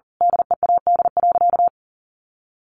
Morse code, heard as a keyed tone, about 40 wpm, 700 Hz. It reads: DEAD.
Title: A Sound Of Thunder 40wpm